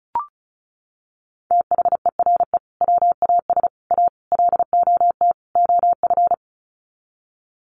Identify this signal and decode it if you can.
Morse code, heard as a keyed tone, about 34 wpm, 700 Hz. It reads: THERE WAS A LOT OF
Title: Sets of 5 Words 35wpm